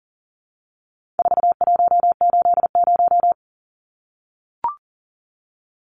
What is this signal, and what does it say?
Morse code, heard as a keyed tone, about 40 wpm, 700 Hz. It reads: 4180
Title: Numbers - 4-Digit 40wpm